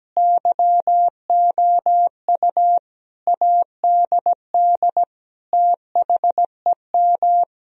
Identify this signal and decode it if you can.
Morse code, heard as a keyed tone, about 17 wpm, 700 Hz. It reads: YOU ADD THEM
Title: Sentences from Top 1000 Words - Review Part 1 - 17wpm